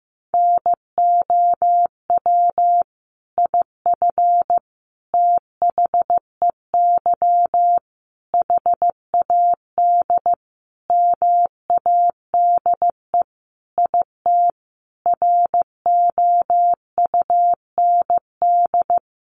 Morse code, heard as a keyed tone, about 15 wpm, 700 Hz. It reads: NOW IF THEY HAD MADE IT ROUND
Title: Sentences from Top 500 Words 15wpm